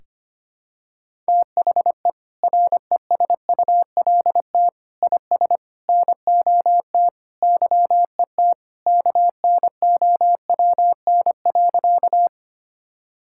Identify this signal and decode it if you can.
Morse code, heard as a keyed tone, about 25 wpm, 700 Hz. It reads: THE RESULT IS NOT YET KNOWN.
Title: Sentences from Top 1000 Words - Review Part 5 - 25wpm